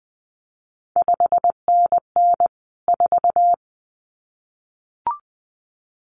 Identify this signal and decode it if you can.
Morse code, heard as a keyed tone, about 20 wpm, 700 Hz. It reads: 5NN 4
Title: CQ WW DX Contest Exchange - 20wpm